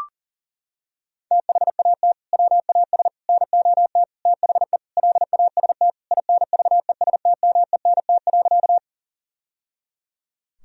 Morse code, heard as a keyed tone, about 40 wpm, 700 Hz. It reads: THAT WAS NOT THE LAST INVESTMENT.
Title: Sentences from Top 1000 Words - Review Part 4 - 40wpm